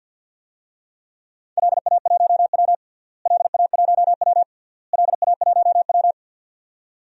Morse code, heard as a keyed tone, about 47 wpm, 700 Hz. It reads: LA1W LA1W LA1W
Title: DXCC-100 Call Signs - 4 Characters - Repeated 3x - 50wpm